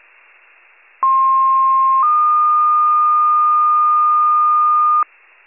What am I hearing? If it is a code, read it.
Two-tone page, heard as a one-second tone, then a three-second tone, about 1060 then 1220 Hz, with noise. It is heard from 1.0 to 5.0 s.